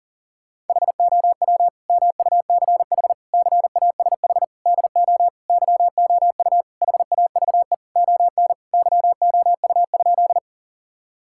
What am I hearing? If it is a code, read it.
Morse code, heard as a keyed tone, about 40 wpm, 700 Hz. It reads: HOW MUCH CASH DO YOU HAVE ON YOU?
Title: Sentences from Top 1000 Words - Review Part 5 - 40wpm